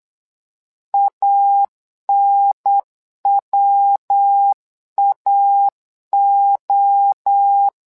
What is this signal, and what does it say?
Morse code, heard as a keyed tone, about 8 wpm, 800 Hz. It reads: ANWAO
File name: LEZIONI / 3 MWRO / 3 MWRO 8wpm 2606